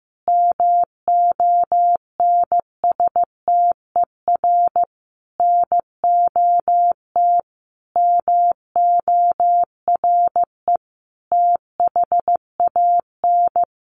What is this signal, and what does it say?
Morse code, heard as a keyed tone, about 15 wpm, 700 Hz. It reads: MONSTER NOT MORE THAN
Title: A Sound Of Thunder 15wpm